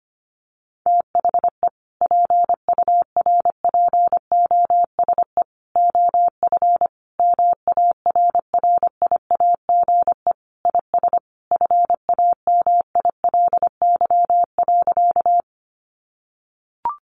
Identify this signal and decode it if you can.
Morse code, heard as a keyed tone, about 25 wpm, 700 Hz. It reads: THE PURPOSE OF MARRIAGE IS FAMILY.
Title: Sentences from Top 800 Words 25wpm